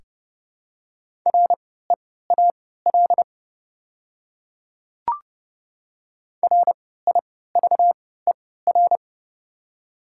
Morse code, heard as a keyed tone, about 32 wpm, spaced as for 17 wpm, 700 Hz. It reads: REAL RIVER
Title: 3x Character Spacing - Top 500 Words - 30wpm